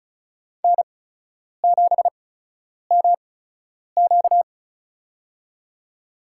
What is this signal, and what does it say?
Morse code, heard as a keyed tone, about 36 wpm, spaced as for 10 wpm, 700 Hz. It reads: N7MQ
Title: USA Call Signs - 4 Characters - 8x Character Spacing - 35wpm